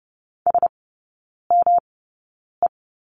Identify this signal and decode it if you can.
Morse code, heard as a keyed tone, about 30 wpm, spaced as for 9 wpm, 700 Hz. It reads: SME
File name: Lesson 18e - Letter m - Three Characters 30wpm